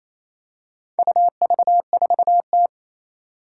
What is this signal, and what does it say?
Morse code, heard as a keyed tone, about 28 wpm, 700 Hz. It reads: UV4T
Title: Four Challenging Letter-Number Combinations 28wpm